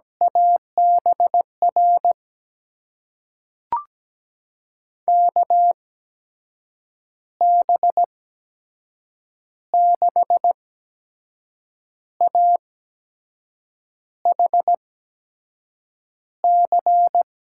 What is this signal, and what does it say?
Morse code, heard as a keyed tone, about 17 wpm, spaced as for 6 wpm, 700 Hz. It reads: ABR KB6AHC
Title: USA Call Signs - 8x Character Spacing - 17wpm